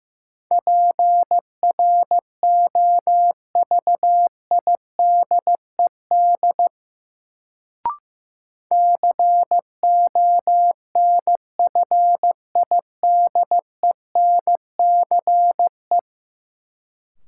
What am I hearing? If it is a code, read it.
Morse code, heard as a keyed tone, about 15 wpm, 700 Hz. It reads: PROVIDED CONFIDENCE
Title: Sentences from Top 1500 Words - New Word Drill - 15wpm